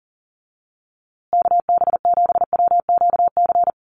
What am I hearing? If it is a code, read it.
Morse code, heard as a keyed tone, about 40 wpm, 700 Hz. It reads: KB7WQC
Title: USA Call Signs Repeated 2x 40wpm